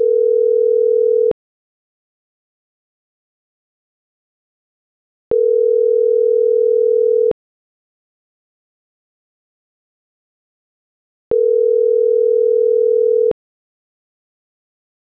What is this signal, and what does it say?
Ringback tone (North America), 440 + 480 Hz.